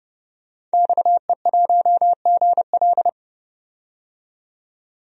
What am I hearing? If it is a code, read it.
Morse code, heard as a keyed tone, about 30 wpm, 700 Hz. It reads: XE1GL
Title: DXCC-100 Call Signs - 5 Characters - 30wpm